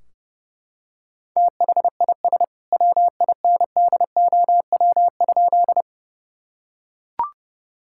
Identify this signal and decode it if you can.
Morse code, heard as a keyed tone, about 32 wpm, 700 Hz. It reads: THIS WINDOW?
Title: Sentences from Top 1600 Words 30wpm